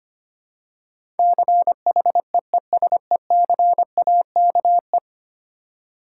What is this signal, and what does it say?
Morse code, heard as a keyed tone, about 25 wpm, 700 Hz. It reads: CHEESECAKE
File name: practice / 25wpm / Difficult Words with Hints - 25wpm